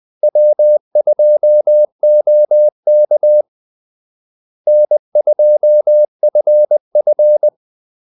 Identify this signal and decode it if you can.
Morse code, heard as a keyed tone, about 20 wpm, 600 Hz. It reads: W2OK N2FF
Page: Kaldesignaler 20 wpm